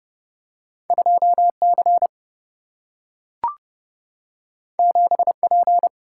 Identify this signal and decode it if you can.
Morse code, heard as a keyed tone, about 30 wpm, 700 Hz. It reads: 2 C 7 P
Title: Single-Letter Single-Number 30wpm